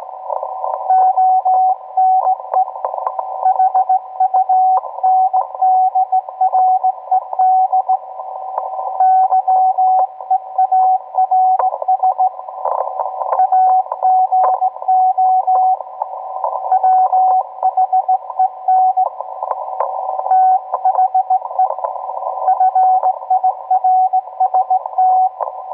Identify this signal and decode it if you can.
Morse code, heard as a keyed tone, about 17 wpm, 750 Hz, with radio noise. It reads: ONE HUNDRED YEARS AGO WHEN THE FIRST